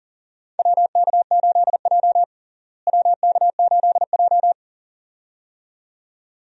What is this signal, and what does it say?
Morse code, heard as a keyed tone, about 40 wpm, 700 Hz. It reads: WK8J WK8J
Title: USA Call Signs - 4 Characters - Repeated 2x - 40wpm